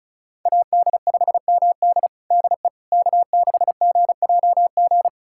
Morse code, heard as a keyed tone, about 34 wpm, 700 Hz. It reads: AD5MD DE K6GJG
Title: USA Callsign DE Callsign - 5 Characters - 35wpm